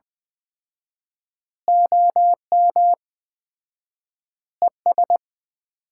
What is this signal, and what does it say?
Morse code, heard as a keyed tone, about 20 wpm, 700 Hz. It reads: OM ES
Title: Common QSO Phrases - 4x Word-Spacing - 20wpm